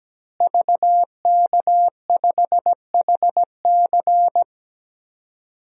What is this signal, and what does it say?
Morse code, heard as a keyed tone, about 17 wpm, 700 Hz. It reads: VK5HC
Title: DXCC-100 Call Signs - 5 Characters - Repeated 3x - 17wpm